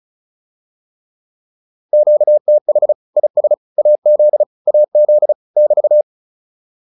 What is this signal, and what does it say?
Morse code, heard as a keyed tone, about 34 wpm, 600 Hz. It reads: QTH IS AZ AZ <BT>